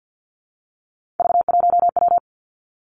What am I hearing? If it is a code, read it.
Morse code, heard as a keyed tone, about 50 wpm, 700 Hz. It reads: V1W
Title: Three Challenging Letter-Number Combinations 50wpm